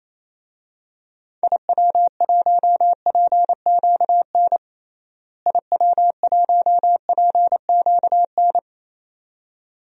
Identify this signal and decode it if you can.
Morse code, heard as a keyed tone, about 28 wpm, 700 Hz. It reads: IW1PQN IW1PQN
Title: DXCC-100 Call Signs - 6 Characters - Repeated 2x - 28wpm